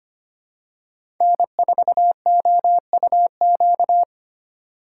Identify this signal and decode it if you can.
Morse code, heard as a keyed tone, about 25 wpm, 700 Hz. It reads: N4OUQ
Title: USA Call Signs - 5 Characters - 2x Character Spacing - 25wpm